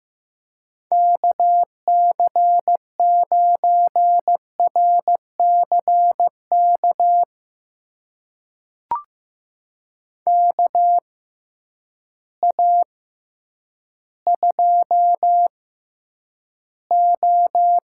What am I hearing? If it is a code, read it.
Morse code, heard as a keyed tone, about 15 wpm, spaced as for 9 wpm, 700 Hz. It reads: KC9RCK KA2O
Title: USA Call Signs - 6 Characters - 6x Character Spacing - 15wpm